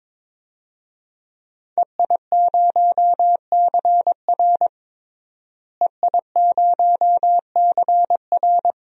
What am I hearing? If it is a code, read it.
Morse code, heard as a keyed tone, about 22 wpm, 700 Hz. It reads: EI0CR EI0CR
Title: DXCC-100 Call Signs - 5 Characters - Repeated 2x - 22wpm